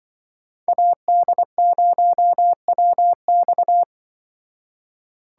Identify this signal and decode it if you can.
Morse code, heard as a keyed tone, about 24 wpm, 700 Hz. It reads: AD0WX
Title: Speed Racing - USA 2x2 Call Signs 20wpm